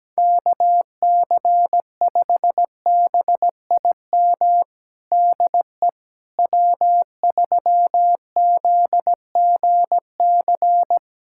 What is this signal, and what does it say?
Morse code, heard as a keyed tone, about 17 wpm, 700 Hz. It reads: KC5BIM DE W3ZGC
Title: USA Callsign DE Callsign 17wpm